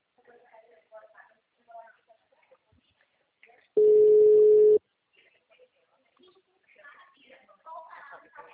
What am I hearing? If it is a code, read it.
Ringback tone (Europe, 425 Hz), with speech.
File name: out- -2010- - - .62242